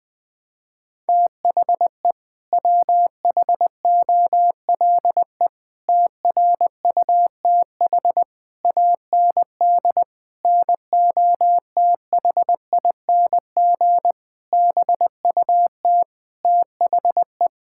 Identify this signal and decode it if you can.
Morse code, heard as a keyed tone, about 20 wpm, 700 Hz. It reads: THE WHOLE TRUTH AND NOTHING BUT THE
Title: Sets of 8 Words - 20wpm